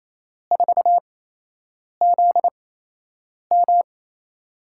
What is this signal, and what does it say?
Morse code, heard as a keyed tone, about 28 wpm, spaced as for 8 wpm, 700 Hz. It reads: 4ZM
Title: USA Call Signs - 8x Character Spacing - 28wpm